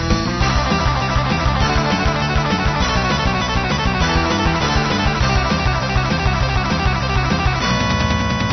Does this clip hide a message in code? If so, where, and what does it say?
No.